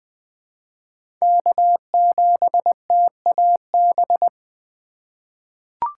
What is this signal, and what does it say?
Morse code, heard as a keyed tone, about 20 wpm, 700 Hz. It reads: K7TAB
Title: 2023 SOTA Chaser Call Signs - 20wpm